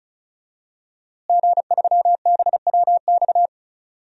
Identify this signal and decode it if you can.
Morse code, heard as a keyed tone, about 36 wpm, 700 Hz. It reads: G3BWX
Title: DXCC-100 Call Signs - 5 Characters - Repeated 3x - 35wpm